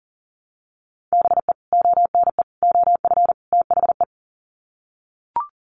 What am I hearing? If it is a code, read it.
Morse code, heard as a keyed tone, about 40 wpm, 700 Hz. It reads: BE ONE OF THE
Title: Sets of 4 Words 40wpm